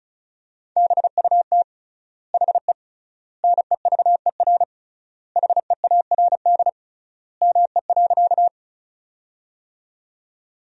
Morse code, heard as a keyed tone, about 36 wpm, 700 Hz. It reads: BUT HE NEVER HEARD ME.
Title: 3x Word-Spacing - Sentences from Top 400 Words - 35wpm